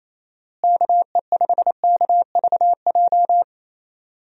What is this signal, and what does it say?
Morse code, heard as a keyed tone, about 28 wpm, 700 Hz. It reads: KE5KVJ
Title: USA Call Signs - 6 Characters - Repeated 3x - 28wpm